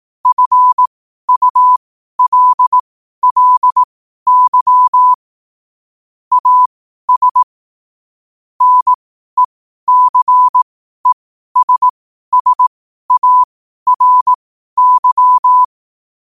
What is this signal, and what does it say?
Morse code, heard as a keyed tone, about 18 wpm, spaced as for 12 wpm, 1000 Hz. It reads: FULLY AS NECESSARY
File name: qotdfiles / 12wpm